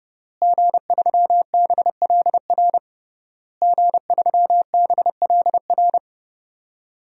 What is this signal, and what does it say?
Morse code, heard as a keyed tone, about 30 wpm, 700 Hz. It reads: G3BLR G3BLR
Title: DXCC-100 Call Signs - 5 Characters - Repeated 3x - 30wpm